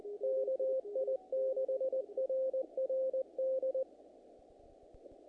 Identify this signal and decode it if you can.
Morse code, heard as a keyed tone, about 20 wpm, 530 Hz. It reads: KI6RRD
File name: contests / naqp-cw-k5tr-op-k5pi- -1900